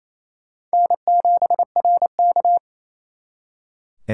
Morse code, heard as a keyed tone, about 34 wpm, 700 Hz. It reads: N7RK
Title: Speed Racing - USA Call Signs 28wpm